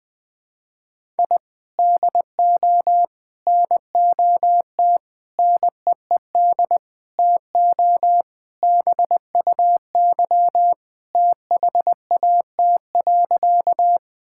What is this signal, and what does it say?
Morse code, heard as a keyed tone, about 20 wpm, 700 Hz. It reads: I DO NOT NEED TO BUY THAT.
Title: Sentences with 5-Word Phrases - Spoken Beginning - 20wpm